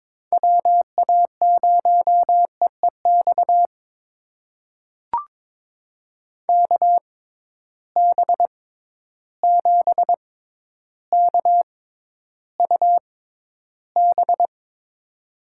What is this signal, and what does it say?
Morse code, heard as a keyed tone, about 22 wpm, spaced as for 11 wpm, 700 Hz. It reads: WA0EEX KB7KUB
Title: USA Call Signs - 6 Characters - 6x Character Spacing - 22wpm